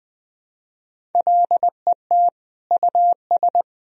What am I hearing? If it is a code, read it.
Morse code, heard as a keyed tone, about 20 wpm, 700 Hz. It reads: LET US
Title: Sentences from Top 1000 Words - Review Part 1 - 20wpm